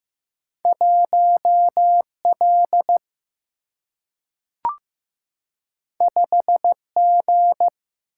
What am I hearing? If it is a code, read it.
Morse code, heard as a keyed tone, about 15 wpm, 700 Hz. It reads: 1L 5G
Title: Single-Letter Single-Number 15wpm